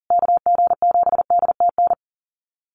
Morse code, heard as a keyed tone, about 40 wpm, 700 Hz. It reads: KG7DTN
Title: USA Call Signs Repeated 2x 40wpm